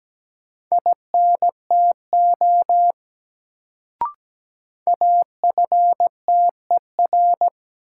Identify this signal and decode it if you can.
Morse code, heard as a keyed tone, about 17 wpm, 700 Hz. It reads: INTO AFTER
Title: Sets of 2 Words - New Word Drill - 17wpm